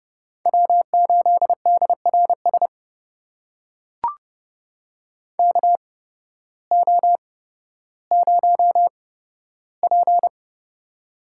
Morse code, heard as a keyed tone, about 30 wpm, spaced as for 14 wpm, 700 Hz. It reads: W8DRS KO0P
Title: USA Call Signs - 8x Character Spacing - 30wpm